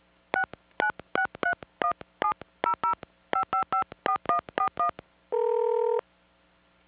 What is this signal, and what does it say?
DTMF, heard as telephone touch tones, tones of about 100 ms. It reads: 696317005554141